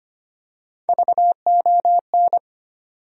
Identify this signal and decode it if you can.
Morse code, heard as a keyed tone, about 25 wpm, 700 Hz. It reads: VON
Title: USA Call Sign Postfixes - 3 Letters 25wpm